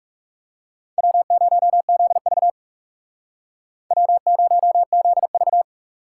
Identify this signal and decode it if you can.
Morse code, heard as a keyed tone, about 34 wpm, 700 Hz. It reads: W0ZV W0ZV
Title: Speed Racing - USA 1x2 Call Signs 30wpm